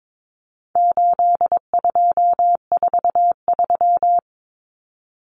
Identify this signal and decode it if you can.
Morse code, heard as a keyed tone, about 22 wpm, 700 Hz. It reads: 8243
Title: Common Cut Numbers - 4-Digit 22wpm